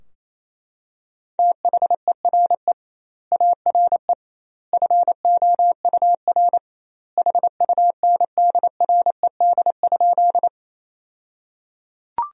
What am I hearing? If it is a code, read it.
Morse code, heard as a keyed tone, about 28 wpm, 700 Hz. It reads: THERE ARE FOUR HUNDRED?
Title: 2x Word-Spacing - Sentences from Top 400 Words - 28wpm